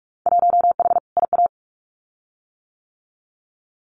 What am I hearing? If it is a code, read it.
Morse code, heard as a keyed tone, about 44 wpm, 700 Hz. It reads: 1H IA
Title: Winter Field Day Exchanges 45wpm